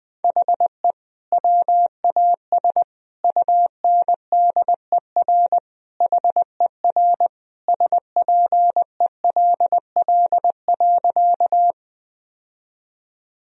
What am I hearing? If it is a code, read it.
Morse code, heard as a keyed tone, about 20 wpm, 700 Hz. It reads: HE WAS UNDER HER SPELL.
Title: Sentences from Top 1000 Words - Review Part 1 - 20wpm